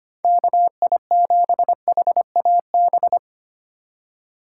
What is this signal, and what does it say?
Morse code, heard as a keyed tone, about 32 wpm, 700 Hz. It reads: KI7HAB